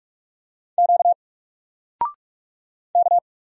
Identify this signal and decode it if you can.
Morse code, heard as a keyed tone, about 46 wpm, 700 Hz. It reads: Q K
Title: Single Character (Letter:Number:Punctuation:Prosign) 45wpm